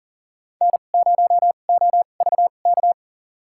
Morse code, heard as a keyed tone, about 40 wpm, spaced as for 29 wpm, 700 Hz. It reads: N0OVK
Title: USA Call Signs - 5 Characters - 2x Character Spacing - 40wpm